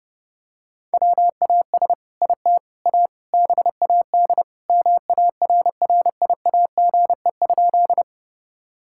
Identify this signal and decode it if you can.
Morse code, heard as a keyed tone, about 30 wpm, 700 Hz. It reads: WAS IT A BAD MARRIAGE?
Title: Sentences from Top 1000 Words - Review Part 5 - 30wpm